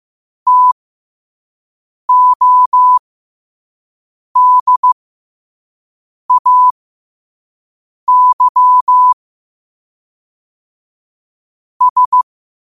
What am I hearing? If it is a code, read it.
Morse code, heard as a keyed tone, about 15 wpm, spaced as for 5 wpm, 1000 Hz. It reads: TODAY S
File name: qotdfiles / 5wpm